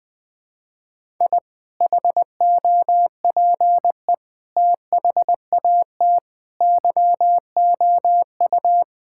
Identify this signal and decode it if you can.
Morse code, heard as a keyed tone, about 21 wpm, 700 Hz. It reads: I HOPE THAT YOU
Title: Sentences with 2-Word Phrases - 20wpm